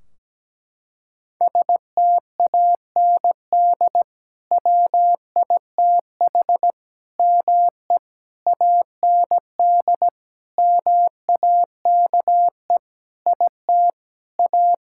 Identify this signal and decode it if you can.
Morse code, heard as a keyed tone, about 17 wpm, 700 Hz. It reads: STAND WITH ME AND MAKE IT A
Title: Sentences from Top 1600 Words 17wpm